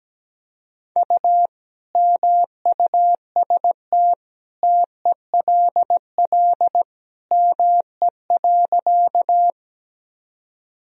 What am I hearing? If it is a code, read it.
Morse code, heard as a keyed tone, about 17 wpm, 700 Hz. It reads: U MUST TELL ME.
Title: Sentences from Top 200 Words and Abbreviations 17wpm